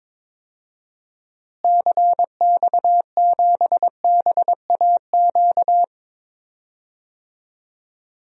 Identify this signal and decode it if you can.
Morse code, heard as a keyed tone, about 22 wpm, 700 Hz. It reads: CX7BAQ
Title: DXCC-100 Call Signs - 6 Characters - Repeated 2x - 22wpm